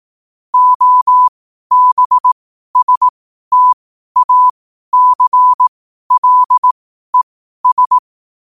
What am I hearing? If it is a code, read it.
Morse code, heard as a keyed tone, about 18 wpm, spaced as for 12 wpm, 1000 Hz. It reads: OBSTACLES
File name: qotdfiles / 12wpm